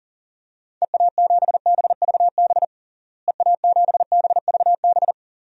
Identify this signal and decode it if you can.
Morse code, heard as a keyed tone, about 37 wpm, 700 Hz. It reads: EA7BVB EA7BVB
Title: DXCC-100 Call Signs - 6 Characters - Repeated 2x - 40wpm